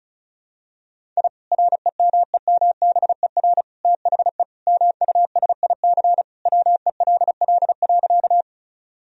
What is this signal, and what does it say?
Morse code, heard as a keyed tone, about 36 wpm, 700 Hz. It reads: I REMEMBER THE MUSIC WELL.
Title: Sentences from Top 1000 Words - Review Part 1 - 35wpm